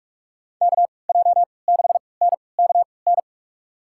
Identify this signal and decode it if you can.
Morse code, heard as a keyed tone, about 46 wpm, spaced as for 25 wpm, 700 Hz. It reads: KJ6NKN
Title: USA Call Signs - 6 Characters - 3x Character Spacing - 45wpm